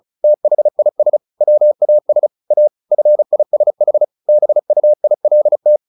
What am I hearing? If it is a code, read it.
Morse code, heard as a keyed tone, about 34 wpm, 600 Hz. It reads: THIS WAS A FISH BUILT
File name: the old man and the sea 2 35wpm